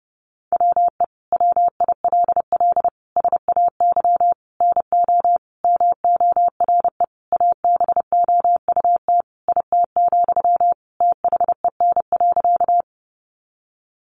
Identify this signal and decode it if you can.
Morse code, heard as a keyed tone, about 30 wpm, 700 Hz. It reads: WE WILL SAY NO MORE ABOUT IT, THEN.
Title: Sentences from Top 200 Words 30wpm